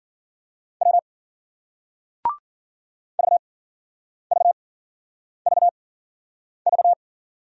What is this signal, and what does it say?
Morse code, heard as a keyed tone, about 56 wpm, 700 Hz. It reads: W V V V V